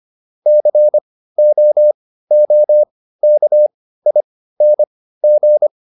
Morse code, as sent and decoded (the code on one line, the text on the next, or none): -.-. --- --- -.- .. -. --.
COOKING